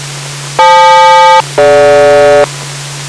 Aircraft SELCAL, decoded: FL-DG